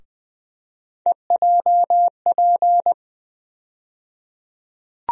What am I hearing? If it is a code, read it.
Morse code, heard as a keyed tone, about 20 wpm, 700 Hz. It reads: EJP
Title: USA Call Sign Postfixes - 3 Letters 20wpm